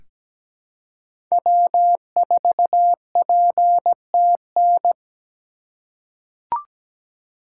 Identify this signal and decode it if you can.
Morse code, heard as a keyed tone, about 17 wpm, 700 Hz. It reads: W4PTN
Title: USA Call Signs - 5 Characters - 3x Character Spacing - 17wpm